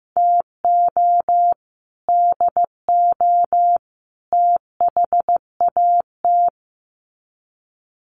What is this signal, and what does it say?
Morse code, heard as a keyed tone, about 15 wpm, 700 Hz. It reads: TO DO THAT
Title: Sets of 3 Words 15wpm